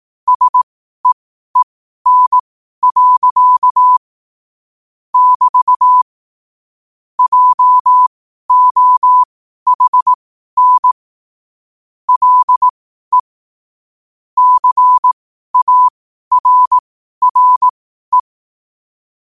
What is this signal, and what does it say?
Morse code, heard as a keyed tone, about 18 wpm, spaced as for 12 wpm, 1000 Hz. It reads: SEEN. <BT> JOHN LE CARRE
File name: qotdfiles / 12wpm